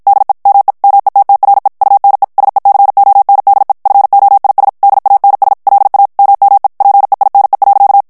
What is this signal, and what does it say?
Morse code, heard as a keyed tone, about 55 wpm, 800 Hz. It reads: DE ME METTRE UNE SECONDE FOIS DANS LA MME PEINE.
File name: FRANCAIS TEXTE CLAIR 2 VITESSE 50 MOTS MINUTES